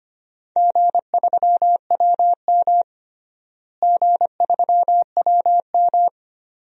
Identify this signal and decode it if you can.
Morse code, heard as a keyed tone, about 25 wpm, 700 Hz. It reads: G3WM G3WM
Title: DXCC-100 Call Signs - 4 Characters - Repeated 2x - 25wpm